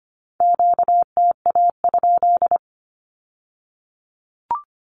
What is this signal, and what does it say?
Morse code, heard as a keyed tone, about 25 wpm, 700 Hz. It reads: QTA?
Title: Q-Codes with Spoken Meaning - 25wpm